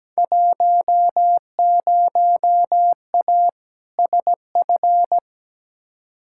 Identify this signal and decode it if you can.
Morse code, heard as a keyed tone, about 17 wpm, 700 Hz. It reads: 10A SF
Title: Field Day Exchanges Encore - 17wpm